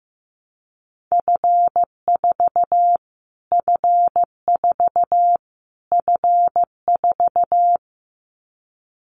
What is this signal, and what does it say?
Morse code, heard as a keyed tone, about 14 wpm, 700 Hz. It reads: F4 F4 F4
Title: Callsign Prefixes for Top 25 DXCC - Repeated 3x - 15wpm